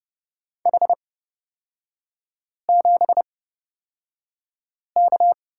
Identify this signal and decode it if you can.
Morse code, heard as a keyed tone, about 30 wpm, spaced as for 5 wpm, 700 Hz. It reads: H7K
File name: Lesson 36e - BT - Three Characters 30wpm